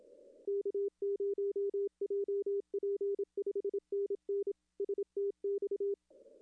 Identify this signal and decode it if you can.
Morse code, heard as a keyed tone, about 26 wpm, 410 Hz. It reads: K0JP 5NN STX
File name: contests / arrl-160-k5tr-op-w5tn -0400